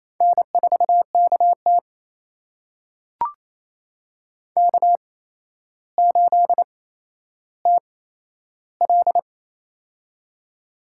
Morse code, heard as a keyed tone, about 28 wpm, spaced as for 12 wpm, 700 Hz. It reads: N4KT K8TL
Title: USA Call Signs - 4 Characters - 8x Character Spacing - 28wpm